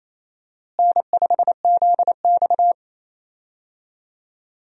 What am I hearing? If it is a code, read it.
Morse code, heard as a keyed tone, about 28 wpm, 700 Hz. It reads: N5ZX
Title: USA Call Signs - 4 Characters - Repeated 3x - 28wpm